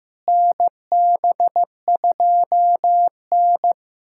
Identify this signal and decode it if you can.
Morse code, heard as a keyed tone, about 16 wpm, 700 Hz. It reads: NB2N
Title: USA Callsign DE Callsign - 4 Characters - 15wpm